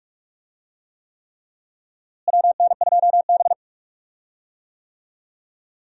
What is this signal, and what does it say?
Morse code, heard as a keyed tone, about 41 wpm, 700 Hz. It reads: WN2B
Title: USA Call Signs - 4 Characters - Repeated 3x - 45wpm